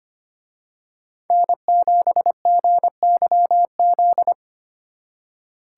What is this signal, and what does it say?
Morse code, heard as a keyed tone, about 29 wpm, 700 Hz. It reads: N7GYZ
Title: Speed Racing - USA 1x3 Call Signs 25wpm